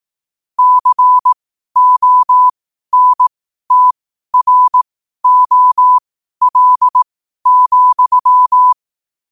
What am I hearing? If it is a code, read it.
Morse code, heard as a keyed tone, about 18 wpm, spaced as for 12 wpm, 1000 Hz. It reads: CONTROL,
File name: qotdfiles / 12wpm